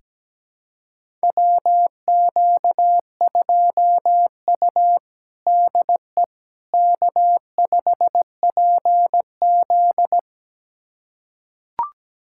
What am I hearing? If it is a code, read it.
Morse code, heard as a keyed tone, about 17 wpm, 700 Hz. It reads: WQ2U DE K5PZ
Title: USA Callsign DE Callsign - 4 Characters - 17wpm